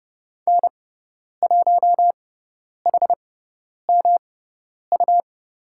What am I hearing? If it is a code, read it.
Morse code, heard as a keyed tone, about 30 wpm, spaced as for 10 wpm, 700 Hz. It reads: N1HMU
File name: Lesson 18g - Letter m - Call Signs 30wpm